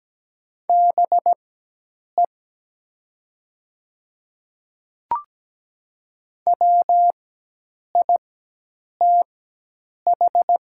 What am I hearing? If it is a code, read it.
Morse code, heard as a keyed tone, about 17 wpm, spaced as for 8 wpm, 700 Hz. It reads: BE WITH
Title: 4x Character Spacing - Top 200 Words - 17wpm